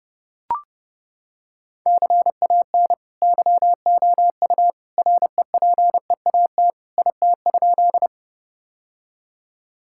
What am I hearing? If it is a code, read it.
Morse code, heard as a keyed tone, about 30 wpm, 700 Hz. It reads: CAN YOU REPEAT IT?
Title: Sentences from Top 1600 Words 30wpm